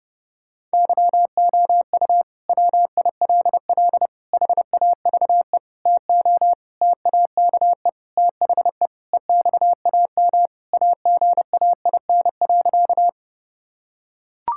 Morse code, heard as a keyed tone, about 32 wpm, 700 Hz. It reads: YOU WILL HAVE TO TAKE THE EXAM AGAIN.
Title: Sentences from Top 1000 Words - Review Part 5 - 30wpm